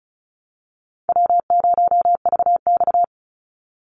Morse code, heard as a keyed tone, about 34 wpm, 700 Hz. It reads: W0VX
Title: Easy Call Signs Repeated 2x - 35wpm